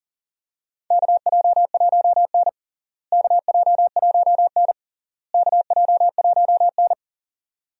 Morse code, heard as a keyed tone, about 40 wpm, 700 Hz. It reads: KJ1N KJ1N KJ1N
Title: USA Call Signs - 4 Characters - Repeated 3x - 40wpm